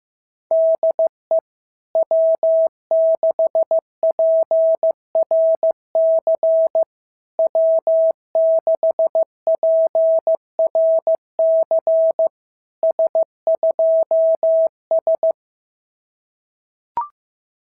Morse code, heard as a keyed tone, about 15 wpm, 650 Hz. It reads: DE W6PRC W6PRC S2S
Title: SOTA QSO Practice 15wpm